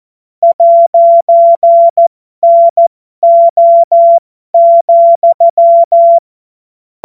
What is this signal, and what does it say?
Morse code, heard as a keyed tone, about 14 wpm, spaced as for 12 wpm, 680 Hz. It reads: 'NO,
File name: Morsekurs / Witze